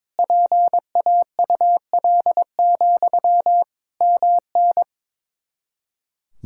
Morse code, heard as a keyed tone, about 22 wpm, 700 Hz. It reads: PAUL, MN
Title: K1USN SST QSO Practice 22wpm